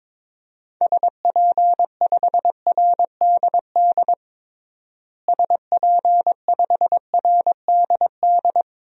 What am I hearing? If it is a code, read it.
Morse code, heard as a keyed tone, about 21 wpm, 700 Hz. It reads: SP5RDD SP5RDD
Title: DXCC-100 Call Signs - 6 Characters - Repeated 2x - 22wpm